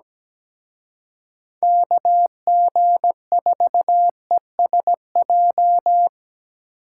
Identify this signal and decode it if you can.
Morse code, heard as a keyed tone, about 20 wpm, 700 Hz. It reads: KG4ESJ
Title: Speed Racing - USA Call Signs 17wpm